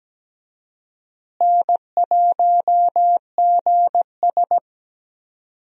Morse code, heard as a keyed tone, about 17 wpm, 700 Hz. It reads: N1GS
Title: USA Call Signs - 4 Characters - Repeated 2x - 17wpm